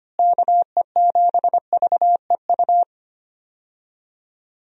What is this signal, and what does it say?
Morse code, heard as a keyed tone, about 25 wpm, 700 Hz. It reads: KE7VEU
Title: USA Call Signs - 6 Characters - Repeated 2x - 25wpm